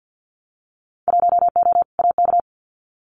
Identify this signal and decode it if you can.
Morse code, heard as a keyed tone, about 50 wpm, 700 Hz. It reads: 1O AK
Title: Winter Field Day Exchanges Encore 50wpm